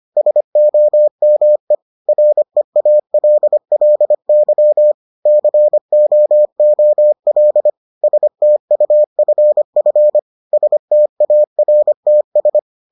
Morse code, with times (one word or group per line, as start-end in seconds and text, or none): 0.2-1.8 SOME
2.1-4.9 REALLY
5.3-7.7 COOL
8.0-10.2 STUFF
10.5-12.6 STARTS